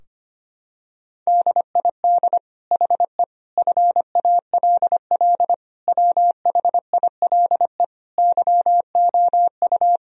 Morse code, heard as a keyed tone, about 25 wpm, 700 Hz. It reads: DID HE FALL WHILE YOU
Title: Sentences from Top 1000 Words - Review Part 2 - 25wpm